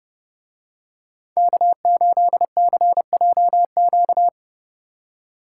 Morse code, heard as a keyed tone, about 29 wpm, 700 Hz. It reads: K8CJQ